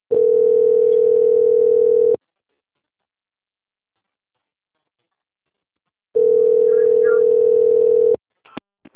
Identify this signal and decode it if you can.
Ringback tone (North America), 440 + 480 Hz.